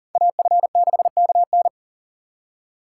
Morse code, heard as a keyed tone, about 40 wpm, 700 Hz. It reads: AF6KN
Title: USA Call Signs - 5 Characters - 6x Character Spacing - 40wpm